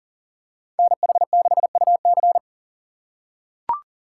Morse code, heard as a keyed tone, about 40 wpm, 700 Hz. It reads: NH6UC
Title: USA Call Signs - 5 Characters - 4x Character Spacing - 40wpm